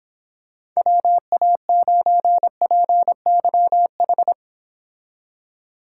Morse code, heard as a keyed tone, about 26 wpm, 700 Hz. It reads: WA9PYH